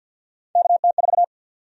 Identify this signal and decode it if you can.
Morse code, heard as a keyed tone, about 47 wpm, 700 Hz. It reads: KT4
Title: USA Call Sign Prefixes - 50wpm